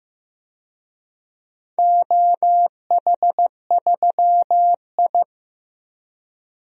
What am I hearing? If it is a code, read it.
Morse code, heard as a keyed tone, about 15 wpm, 700 Hz. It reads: OH3I
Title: DXCC-100 Call Signs - 4 Characters - 15wpm